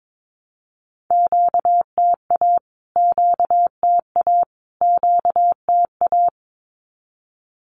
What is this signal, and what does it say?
Morse code, heard as a keyed tone, about 22 wpm, 700 Hz. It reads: QTA QTA QTA
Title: Q-Codes Repeated 3x 22wpm